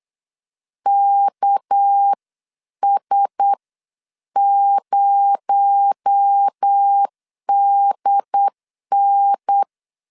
Morse code, heard as a keyed tone, about 8 wpm, spaced as for 7 wpm, 800 Hz. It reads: KS0DN
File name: 8wpm